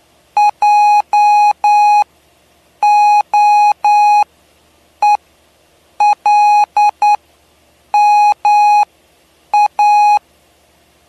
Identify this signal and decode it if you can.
Morse code, heard as a keyed tone, about 9 wpm, spaced as for 7 wpm, 830 Hz. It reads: JOELMA
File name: cw / py2lcd / 07-PY2LCD-Li��o 7